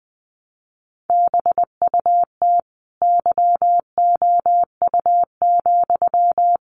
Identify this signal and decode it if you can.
Morse code, heard as a keyed tone, about 19 wpm, 700 Hz. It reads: BUT YOU,
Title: A Sound Of Thunder 20wpm